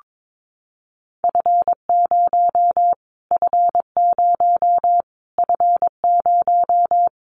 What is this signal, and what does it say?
Morse code, heard as a keyed tone, about 22 wpm, 700 Hz. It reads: F0 F0 F0
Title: Callsign Prefixes for Top 25 DXCC - Repeated 3x - 22wpm